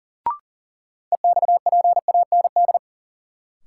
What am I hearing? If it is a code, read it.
Morse code, heard as a keyed tone, about 40 wpm, 700 Hz. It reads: EXPAND
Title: Sentences from Top 1400 Words - New Word Drill Alt - 40wpm